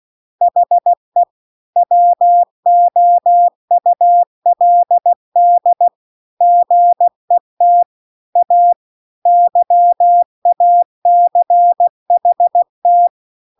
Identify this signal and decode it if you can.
Morse code, heard as a keyed tone, about 16 wpm, 700 Hz. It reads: HE WOULD GET A YACHT